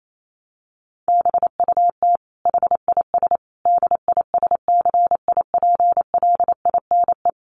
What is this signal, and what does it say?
Morse code, heard as a keyed tone, about 28 wpm, 700 Hz. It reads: BUT HIS DISCIPLINE
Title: Sentences from Top 1400 Words 28wpm